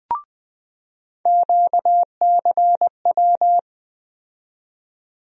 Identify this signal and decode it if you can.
Morse code, heard as a keyed tone, about 20 wpm, 700 Hz. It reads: QCW
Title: USA Call Sign Postfixes - 3 Letters 20wpm